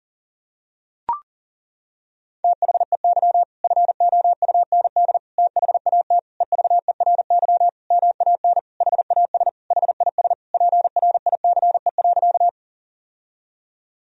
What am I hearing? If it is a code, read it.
Morse code, heard as a keyed tone, about 37 wpm, 700 Hz. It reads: THEY FOUND THAT EVERY MAN HAS HIS PRICE.
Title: Sentences from Top 1000 Words - Review Part 4 - 40wpm